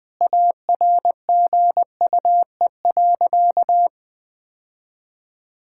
Morse code, heard as a keyed tone, about 20 wpm, 700 Hz. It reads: ARGUE.
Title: Sentences with 7-Word Phrases - 20wpm